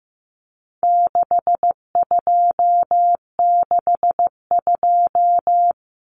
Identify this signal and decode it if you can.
Morse code, heard as a keyed tone, about 15 wpm, 700 Hz. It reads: 6262
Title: Numbers - 4-Digit 15wpm